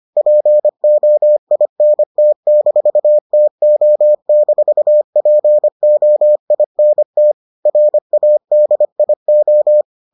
Morse code, heard as a keyed tone, about 25 wpm, 600 Hz. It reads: POINT-TO-POINT RADIO